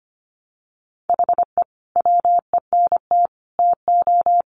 Morse code, heard as a keyed tone, about 25 wpm, 700 Hz. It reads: HE WENT TO
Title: Sentences from Top 1300 Words 25wpm